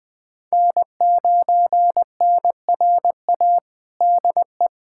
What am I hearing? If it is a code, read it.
Morse code, heard as a keyed tone, about 21 wpm, 700 Hz. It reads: N9NRA DE
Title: USA Callsign DE Callsign 20wpm